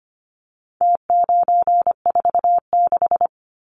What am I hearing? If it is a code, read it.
Morse code, heard as a keyed tone, about 25 wpm, 700 Hz. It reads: T946
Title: Numbers - Contest Style - 4-Digit 25wpm